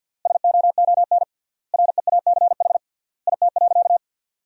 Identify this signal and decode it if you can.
Morse code, heard as a keyed tone, about 52 wpm, 700 Hz. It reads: SOON REACH IT.
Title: 3x Word-Spacing - Sentences from Top 400 Words - 50wpm